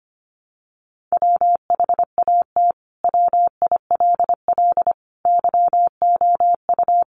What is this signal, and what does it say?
Morse code, heard as a keyed tone, about 25 wpm, 700 Hz. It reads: WHAT WILL YOU
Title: Sentences from Top 1300 Words 25wpm